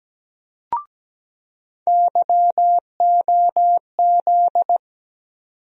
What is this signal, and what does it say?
Morse code, heard as a keyed tone, about 17 wpm, 700 Hz. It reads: YOZ
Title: USA Call Sign Postfixes - 3 Letters 17wpm